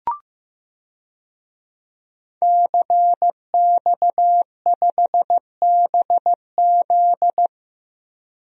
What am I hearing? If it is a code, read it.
Morse code, heard as a keyed tone, about 15 wpm, 700 Hz. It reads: CX5BZ
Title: DXCC-100 Call Signs - 5 Characters - Repeated 2x - 15wpm